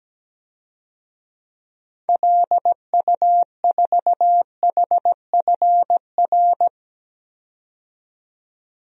Morse code, heard as a keyed tone, about 17 wpm, 700 Hz. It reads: LU4HFR
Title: DXCC-100 Call Signs - 6 Characters - 17wpm